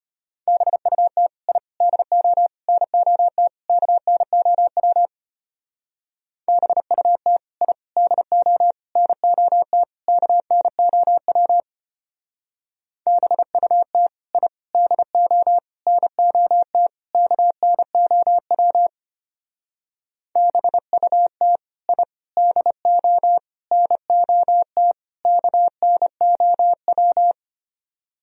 Morse code, heard as a keyed tone, about 29 wpm, 700 Hz. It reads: BUT I DO NOT KNOW BUT I DO NOT KNOW BUT I DO NOT KNOW BUT I DO NOT KNOW
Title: Speed-Racing - Sets of 5 Words 25wpm